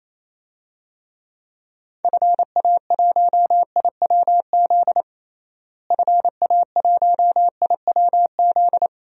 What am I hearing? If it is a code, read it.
Morse code, heard as a keyed tone, about 28 wpm, 700 Hz. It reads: FA1IWZ FA1IWZ
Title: DXCC-100 Call Signs - 6 Characters - Repeated 3x - 28wpm